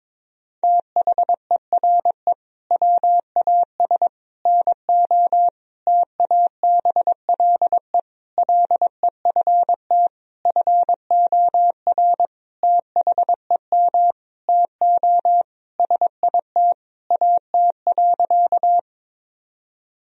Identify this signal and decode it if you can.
Morse code, heard as a keyed tone, about 22 wpm, 700 Hz. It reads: THERE WAS NO TABLE LEFT FOR THEM TO SIT AT.
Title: Sentences from Top 1000 Words - Review Part 2 - 22wpm